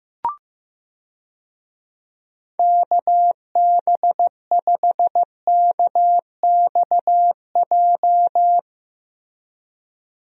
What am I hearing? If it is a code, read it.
Morse code, heard as a keyed tone, about 16 wpm, 700 Hz. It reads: KB5KXJ
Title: USA Call Signs - 6 Characters - Repeated 3x - 15wpm